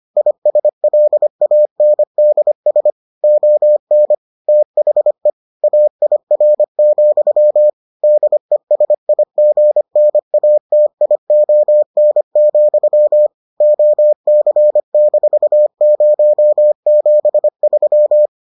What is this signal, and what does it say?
Morse code, heard as a keyed tone, about 25 wpm, 600 Hz. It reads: ISLANDS ON THE AIR, DESIGNATION, OC-073